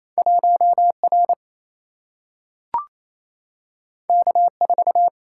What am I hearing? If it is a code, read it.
Morse code, heard as a keyed tone, about 28 wpm, 700 Hz. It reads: 1 R K 4
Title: Single-Letter Single-Number 28wpm